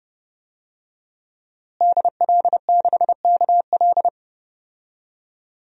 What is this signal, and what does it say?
Morse code, heard as a keyed tone, about 32 wpm, 700 Hz. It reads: DL6KL
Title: DXCC-100 Call Signs - 5 Characters - Repeated 3x - 30wpm